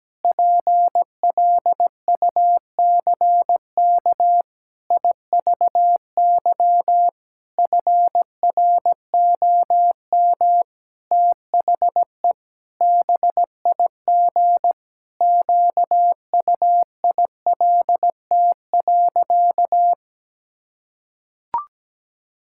Morse code, heard as a keyed tone, about 17 wpm, 700 Hz. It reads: PLUCK IVY FROM THE BIG QUILT.
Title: Pangram Challenge v2 - 17wpm